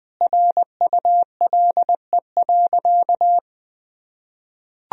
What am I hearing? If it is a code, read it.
Morse code, heard as a keyed tone, about 21 wpm, 700 Hz. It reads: RULE.
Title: Sentences from Top 1000 Words - Review Part 2 - 20wpm